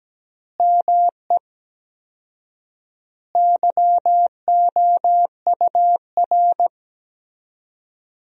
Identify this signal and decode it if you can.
Morse code, heard as a keyed tone, about 17 wpm, 700 Hz. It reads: ME YOUR
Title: 4x Word-Spacing - Sentences from Top 200 Words - 17wpm